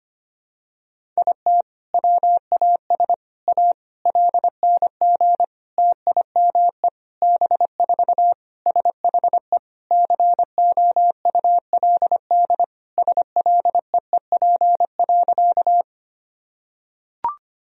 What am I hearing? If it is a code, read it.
Morse code, heard as a keyed tone, about 25 wpm, 700 Hz. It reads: IT WAS A LNG TIME B4 SHE COULD SLEEP.
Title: Sentences from Top 500 Words and Abbreviations 25wpm